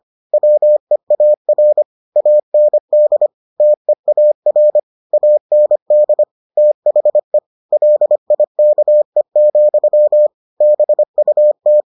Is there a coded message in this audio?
Yes: Morse code from 0.3 to 11.8 s, heard as a keyed tone, about 25 wpm, 600 Hz. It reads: WEAR AND TEAR AND THE LIKE, BUT